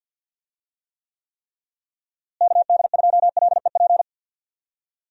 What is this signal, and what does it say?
Morse code, heard as a keyed tone, about 52 wpm, 700 Hz. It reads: KD2LEP
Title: USA Call Signs - 6 Characters - Repeated 3x - 50wpm